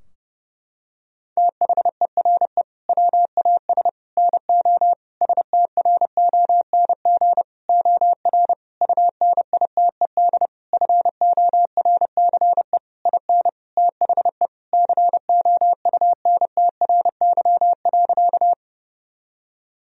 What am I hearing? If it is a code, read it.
Morse code, heard as a keyed tone, about 30 wpm, 700 Hz. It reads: THERE WAS NO STRONG OR UNITED FORCE IN THE COUNTRY.
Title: Sentences from Top 1000 Words - Review Part 4 - 30wpm